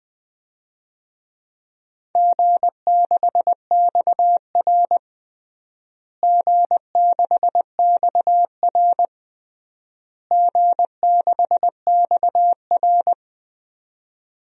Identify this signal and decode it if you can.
Morse code, heard as a keyed tone, about 21 wpm, 700 Hz. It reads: G6XR G6XR G6XR
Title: DXCC-100 Call Signs - 4 Characters - Repeated 3x - 20wpm